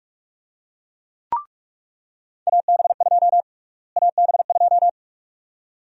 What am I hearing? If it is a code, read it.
Morse code, heard as a keyed tone, about 41 wpm, 700 Hz. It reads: AB2 AB2
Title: USA Call Sign Prefixes - Repeated 2x - 45wpm